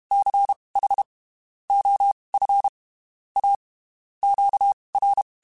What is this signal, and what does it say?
Morse code, heard as a keyed tone, about 32 wpm, spaced as for 23 wpm, 800 Hz. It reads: CH OF A QR